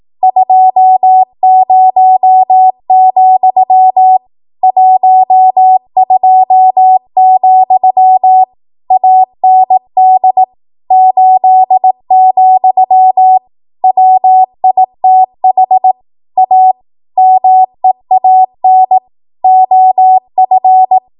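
Morse code, heard as a keyed tone, about 18 wpm, 750 Hz. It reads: 20, 12, AND 8, WITH A MEAN OF
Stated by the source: ARRL 18 wpm Morse code practice archive transmission.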